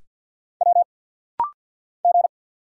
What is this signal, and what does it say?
Morse code, heard as a keyed tone, about 50 wpm, 700 Hz. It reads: W G
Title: Single Letters - Rapid-Fire 50wpm